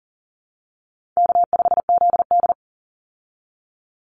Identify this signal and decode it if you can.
Morse code, heard as a keyed tone, about 40 wpm, 700 Hz. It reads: K5ZD
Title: Easy Call Signs Repeated 2x - 40wpm